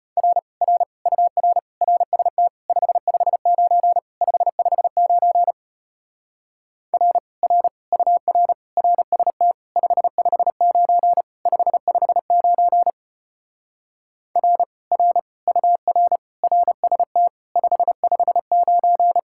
Morse code, heard as a keyed tone, about 33 wpm, 700 Hz. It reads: R R UR RST 559 559 R R UR RST 559 559 R R UR RST 559